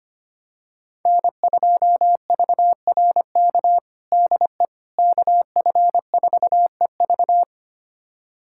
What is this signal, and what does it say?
Morse code, heard as a keyed tone, about 25 wpm, 700 Hz. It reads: N2VRK DE KF4EV
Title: USA Callsign DE Callsign - 5 Characters - 25wpm